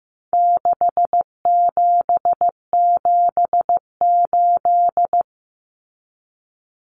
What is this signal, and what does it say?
Morse code, heard as a keyed tone, about 15 wpm, 700 Hz. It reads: 6778
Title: Numbers - 4-Digit 15wpm